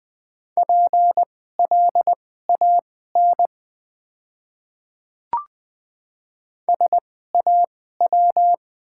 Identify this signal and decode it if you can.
Morse code, heard as a keyed tone, about 21 wpm, spaced as for 15 wpm, 700 Hz. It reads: PLAN SAW
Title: 2x Character Spacing - Top 500 Words - 20wpm